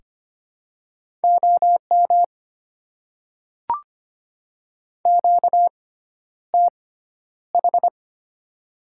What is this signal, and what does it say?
Morse code, heard as a keyed tone, about 25 wpm, spaced as for 11 wpm, 690 Hz. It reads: OM QTH
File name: Top 90 QSO Elements - 6x Character Spacing - 25wpm